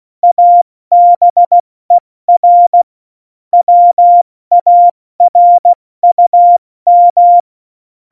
Morse code, heard as a keyed tone, about 16 wpm, 700 Hz. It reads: ABER WARUM
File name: Morsekurs / Witze / Witz 06 March 2018, 70 F80